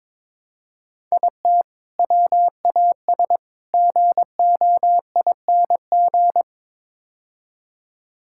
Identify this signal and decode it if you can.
Morse code, heard as a keyed tone, about 22 wpm, 700 Hz. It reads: IT WAS GOING
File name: practice / 22wpm / Build A Phrase - 5-Word Phrases - 22wpm